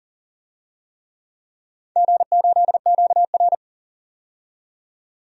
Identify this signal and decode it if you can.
Morse code, heard as a keyed tone, about 39 wpm, 700 Hz. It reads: G8QR
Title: DXCC-100 Call Signs - 4 Characters - 40wpm